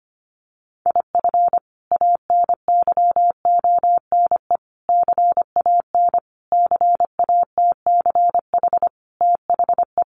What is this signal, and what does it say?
Morse code, heard as a keyed tone, about 25 wpm, 700 Hz. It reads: IF ANYONE CAN CATCH THE
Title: Sentences from Top 1500 Words 25wpm